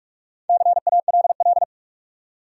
Morse code, heard as a keyed tone, about 45 wpm, 700 Hz. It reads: CALL
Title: Top 100 Words 45wpm